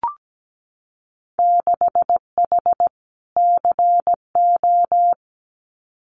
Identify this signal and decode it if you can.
Morse code, heard as a keyed tone, about 17 wpm, 700 Hz. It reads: 6H CO
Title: Winter Field Day Exchanges Encore 17wpm